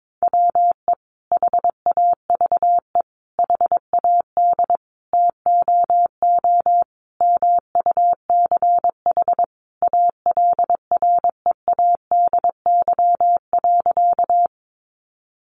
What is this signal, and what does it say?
Morse code, heard as a keyed tone, about 22 wpm, 700 Hz. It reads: WE HAVE HAD TOO MUCH ALREADY.
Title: Sentences from Top 700 Words 22wpm